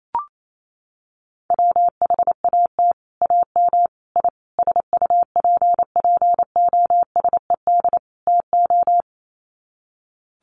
Morse code, heard as a keyed tone, about 28 wpm, 700 Hz. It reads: WHAT AM I SUPPOSED TO
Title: Sets of 5 Words Encore2 28wpm